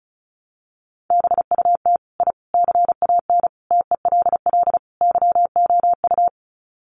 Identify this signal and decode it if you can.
Morse code, heard as a keyed tone, about 34 wpm, 700 Hz. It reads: BUT I CAN TELL YOU
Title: Sets of 5 Words Encore 35wpm